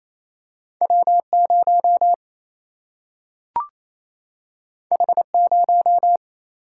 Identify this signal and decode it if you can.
Morse code, heard as a keyed tone, about 28 wpm, 700 Hz. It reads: W0 H0
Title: Single-Letter Single-Number 28wpm